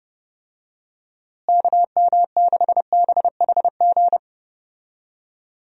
Morse code, heard as a keyed tone, about 31 wpm, 700 Hz. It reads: KM6BHG
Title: USA Call Signs - 6 Characters - Repeated 3x - 30wpm